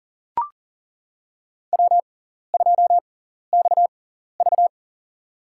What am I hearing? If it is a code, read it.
Morse code, heard as a keyed tone, about 40 wpm, spaced as for 14 wpm, 700 Hz. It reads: W2XV
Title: USA Call Signs - 4 Characters - 6x Character Spacing - 40wpm